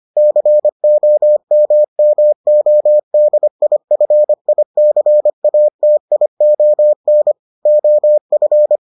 Morse code, heard as a keyed tone, about 25 wpm, 600 Hz. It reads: COMMODIFICATION OF